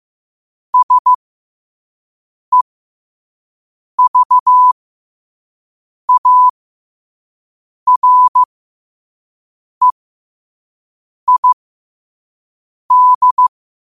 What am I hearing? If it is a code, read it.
Morse code, heard as a keyed tone, about 15 wpm, spaced as for 5 wpm, 1000 Hz. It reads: SEVAREID
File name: qotdfiles / 5wpm